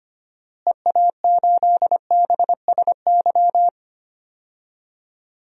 Morse code, heard as a keyed tone, about 25 wpm, 700 Hz. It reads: EA8BSY
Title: DXCC-100 Call Signs - 6 Characters - Repeated 2x - 25wpm